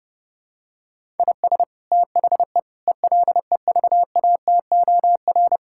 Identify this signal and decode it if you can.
Morse code, heard as a keyed tone, about 32 wpm, 700 Hz. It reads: IS THE ELEVATOR
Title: Sentences from Top 1600 Words 30wpm